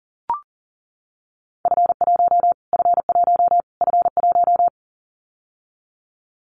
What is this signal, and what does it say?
Morse code, heard as a keyed tone, about 40 wpm, 700 Hz. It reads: F1 F1 F1
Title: Callsign Prefixes for Top 25 DXCC - Repeated 3x - 40wpm